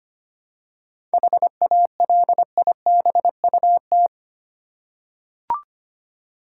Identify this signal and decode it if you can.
Morse code, heard as a keyed tone, about 25 wpm, 700 Hz. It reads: HALIBUT
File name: practice / 25wpm / Difficult Words with Hints - 25wpm